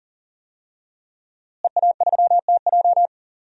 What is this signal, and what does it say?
Morse code, heard as a keyed tone, about 40 wpm, 700 Hz. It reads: EA3TJ
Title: DXCC-100 Call Signs - 5 Characters - Repeated 3x - 40wpm